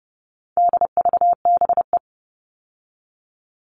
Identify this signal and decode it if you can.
Morse code, heard as a keyed tone, about 30 wpm, 700 Hz. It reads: DVBE
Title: All Cut Numbers - 4-Digit 30wpm